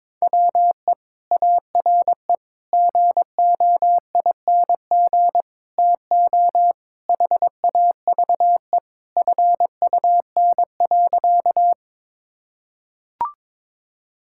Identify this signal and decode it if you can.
Morse code, heard as a keyed tone, about 22 wpm, 700 Hz. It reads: WE ARE GOING TO HAVE FUN.
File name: practice / 22wpm / Sentences with 4-Word Phrases - 22wpm